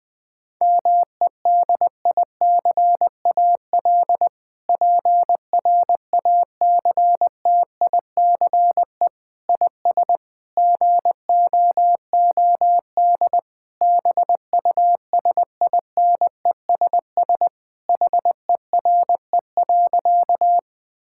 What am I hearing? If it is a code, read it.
Morse code, heard as a keyed tone, about 20 wpm, 700 Hz. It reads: MEDICAL PRACTICE IS GOOD BUSINESS HERE.
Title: Sentences from Top 1000 Words - Review Part 5 - 20wpm